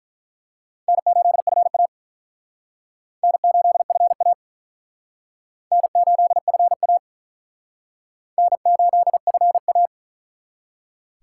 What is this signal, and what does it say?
Morse code, heard as a keyed tone, about 44 wpm, 700 Hz. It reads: N8FA N8FA N8FA N8FA
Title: Speed Racing - USA 1x2 Call Signs 35wpm